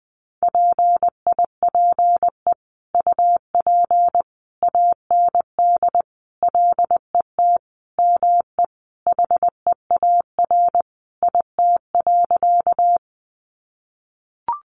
Morse code, heard as a keyed tone, about 20 wpm, 700 Hz. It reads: PIPE UP AND LET ME HEAR IT.
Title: Sentences from Top 1500 Words 20wpm